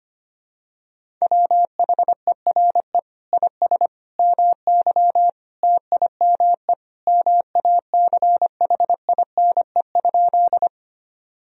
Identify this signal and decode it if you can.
Morse code, heard as a keyed tone, about 25 wpm, 700 Hz. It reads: WHERE IS MY TIME MACHINE?
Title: Sentences from Top 1000 Words - Review Part 2 - 25wpm